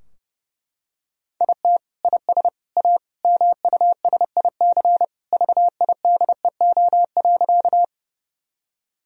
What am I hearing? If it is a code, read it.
Morse code, heard as a keyed tone, about 30 wpm, 700 Hz. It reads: IT IS A MUSIC VIDEO.
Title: Sentences from Top 1000 Words - Review Part 4 - 30wpm